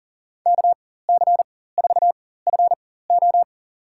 Morse code, heard as a keyed tone, about 37 wpm, spaced as for 18 wpm, 700 Hz. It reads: KC4FO
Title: USA Call Signs - 5 Characters - 4x Character Spacing - 40wpm